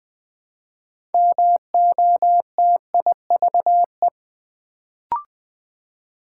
Morse code, heard as a keyed tone, about 20 wpm, 700 Hz. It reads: MOTIVE
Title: Build A Word - 3 Letters At A Time - 6-Letter Words - 20wpm